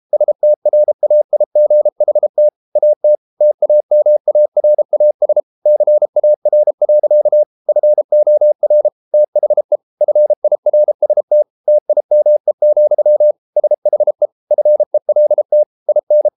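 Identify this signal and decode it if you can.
Morse code, heard as a keyed tone, about 33 wpm, 600 Hz. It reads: STRAIGHT AT TAMARAS CAR. FOR THE FIRST TIME, SHE FELT IN
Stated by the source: Morse code practice at 32 wpm from CW med Gnister, reading Ken Follett's novel Never in English.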